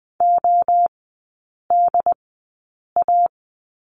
Morse code, heard as a keyed tone, about 20 wpm, spaced as for 8 wpm, 700 Hz. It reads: ODA
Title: 074 Lesson 13e - Letter L - Three Characters 20wpm